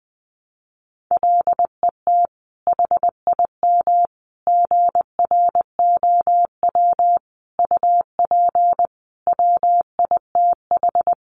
Morse code, heard as a keyed tone, about 20 wpm, 700 Hz. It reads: LET HIM GROW UP WITH
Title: Sentences from Top 300 Words 20wpm